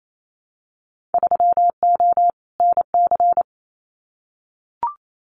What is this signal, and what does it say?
Morse code, heard as a keyed tone, about 28 wpm, 700 Hz. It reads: 3O NC
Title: Winter Field Day Exchanges 28wpm